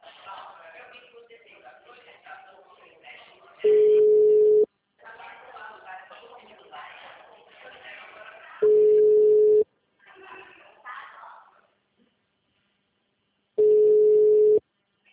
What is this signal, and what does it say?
Ringback tone (Europe, 425 Hz), with speech.
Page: Piracicaba